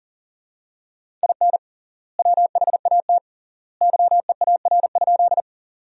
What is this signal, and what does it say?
Morse code, heard as a keyed tone, about 40 wpm, 700 Hz. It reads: IN WHAT YEAR?
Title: 3x Word-Spacing - Sentences from Top 200 Words - 40wpm